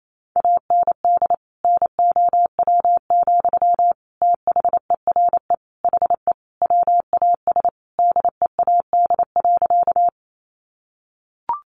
Morse code, heard as a keyed tone, about 28 wpm, 700 Hz. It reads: AND NOW, THERE HE WAS DEAD.
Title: Sentences from Top 1500 Words 28wpm